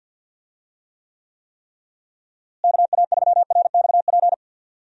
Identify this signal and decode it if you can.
Morse code, heard as a keyed tone, about 52 wpm, 700 Hz. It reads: KA3RXP
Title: USA Call Signs - 6 Characters - Repeated 3x - 50wpm